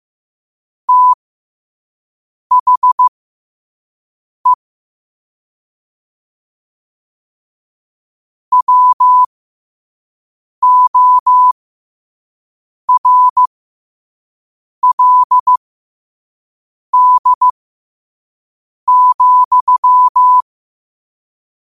Morse code, heard as a keyed tone, about 15 wpm, spaced as for 5 wpm, 1000 Hz. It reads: THE WORLD,
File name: qotdfiles / 5wpm